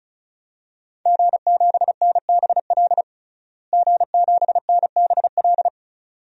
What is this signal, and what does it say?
Morse code, heard as a keyed tone, about 36 wpm, 700 Hz. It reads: G7NBL G7NBL
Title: DXCC-100 Call Signs - 5 Characters - Repeated 2x - 35wpm